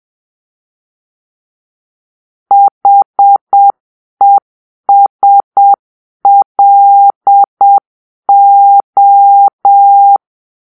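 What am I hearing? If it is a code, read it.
Morse code, heard as a keyed tone, about 7 wpm, 800 Hz. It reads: HESLO
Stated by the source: Morse-code cipher made by a player team for the Kachna puzzle game.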